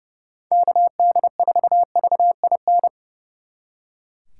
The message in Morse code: KD4VIN

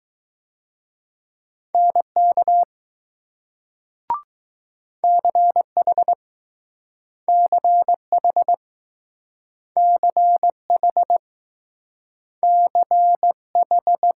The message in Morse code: NK CH CH CH CH

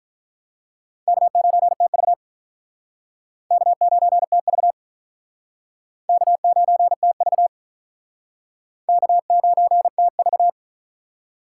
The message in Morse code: K9TV K9TV K9TV K9TV

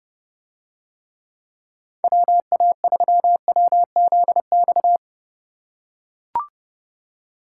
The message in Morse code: WA3WZX

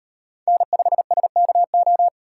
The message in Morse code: N5SKO

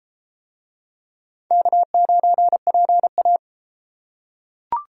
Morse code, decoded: K9PA

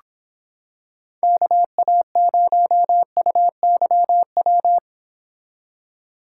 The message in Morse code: KA0UYW

